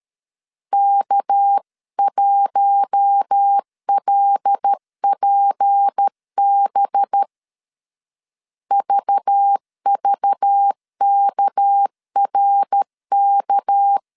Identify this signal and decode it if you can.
Morse code, heard as a keyed tone, about 13 wpm, 800 Hz. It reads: K1LPB VVKRK
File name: CW / DET / 1 6 1500C S B 12wpm 3431